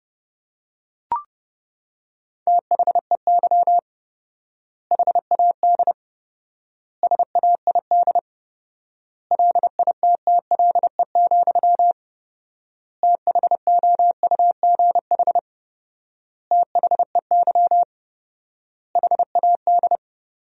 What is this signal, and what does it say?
Morse code, heard as a keyed tone, about 30 wpm, 700 Hz. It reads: THEY HAD SAID LITTLE, THOUGH THEY HAD
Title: 4x Word-Spacing - Sentences from Top 400 Words - 30wpm